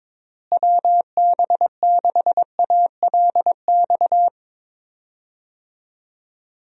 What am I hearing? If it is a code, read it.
Morse code, heard as a keyed tone, about 22 wpm, 700 Hz. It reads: WB6ALX
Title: USA Call Signs - 6 Characters - Repeated 3x - 22wpm